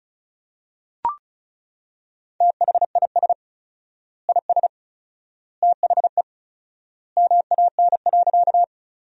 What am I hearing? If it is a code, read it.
Morse code, heard as a keyed tone, about 36 wpm, 700 Hz. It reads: THIS IS THE MAN.
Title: 4x Word-Spacing - Sentences from Top 200 Words - 35wpm